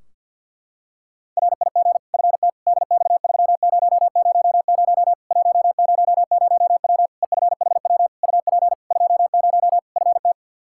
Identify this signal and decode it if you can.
Morse code, heard as a keyed tone, about 48 wpm, 700 Hz. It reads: RIG FT DX3000 100W EFHW UP 20 FT
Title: K1USN SST QSO Practice 50wpm